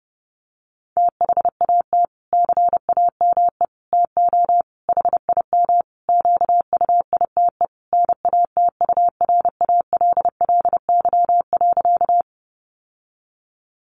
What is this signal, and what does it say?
Morse code, heard as a keyed tone, about 28 wpm, 700 Hz. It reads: THAT CAME TO HIM QUITE NATURALLY.
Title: Sentences from Top 1300 Words 30wpm